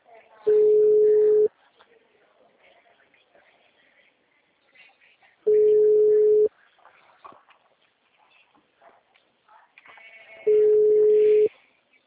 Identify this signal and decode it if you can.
Ringback tone (Europe, 425 Hz), with speech.